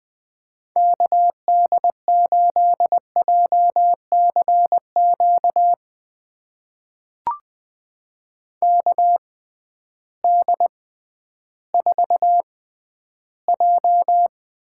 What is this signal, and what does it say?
Morse code, heard as a keyed tone, about 20 wpm, spaced as for 12 wpm, 700 Hz. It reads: KD8JCQ KD4J
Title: USA Call Signs - 6 Characters - 6x Character Spacing - 20wpm